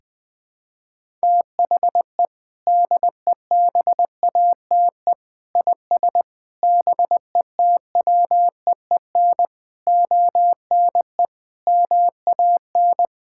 Morse code, heard as a keyed tone, about 21 wpm, 700 Hz. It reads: THE DEBATE IS BETWEEN ONE MAN
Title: Sentences with 6-Word Phrases - 20wpm